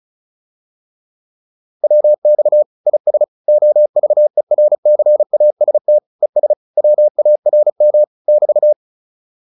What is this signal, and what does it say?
Morse code, heard as a keyed tone, about 34 wpm, 600 Hz. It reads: WX IS OVERCAST ES WARM <BT>